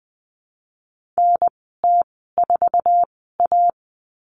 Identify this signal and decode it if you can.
Morse code, heard as a keyed tone, about 20 wpm, spaced as for 14 wpm, 700 Hz. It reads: NT4A
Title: 076 Lesson 13g - Letter L - Call Signs 20wpm